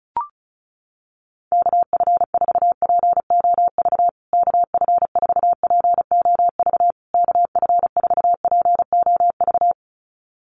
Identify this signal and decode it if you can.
Morse code, heard as a keyed tone, about 34 wpm, 700 Hz. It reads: KF4POV KF4POV KF4POV
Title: USA Call Signs Repeated 3x 35wpm